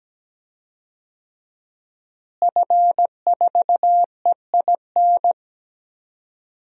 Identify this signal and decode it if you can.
Morse code, heard as a keyed tone, about 17 wpm, 700 Hz. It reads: F4EIN
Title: DXCC-100 Call Signs - 5 Characters - Repeated 2x - 17wpm